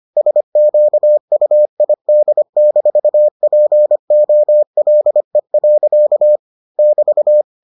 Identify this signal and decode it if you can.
Morse code, heard as a keyed tone, about 25 wpm, 600 Hz. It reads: SQUID-POLE. <BT>